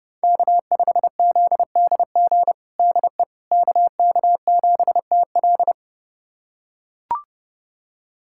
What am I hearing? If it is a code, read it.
Morse code, heard as a keyed tone, about 30 wpm, 700 Hz. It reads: K5ZDG DE KK7TL
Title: USA Callsign DE Callsign - 5 Characters - 30wpm